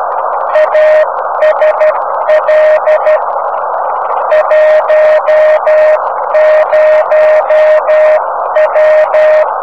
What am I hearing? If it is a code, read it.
Morse code, heard as a keyed tone, about 12 wpm, 620 Hz, with noise. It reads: ASL 10W
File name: om3pv 70mhz / zaznam / hg1bvc